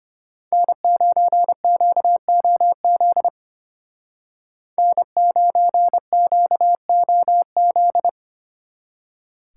Morse code, heard as a keyed tone, about 32 wpm, 700 Hz. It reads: N9QOZ N9QOZ